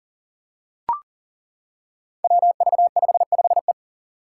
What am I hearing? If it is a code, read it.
Morse code, heard as a keyed tone, about 42 wpm, 700 Hz. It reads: WV55E
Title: Five Challenging Letter-Number Combinations 40wpm